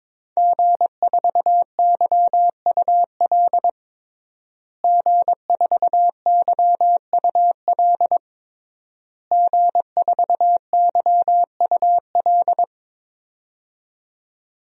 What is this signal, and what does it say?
Morse code, heard as a keyed tone, about 22 wpm, 700 Hz. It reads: G4YUL G4YUL G4YUL
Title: DXCC-100 Call Signs - 5 Characters - Repeated 3x - 22wpm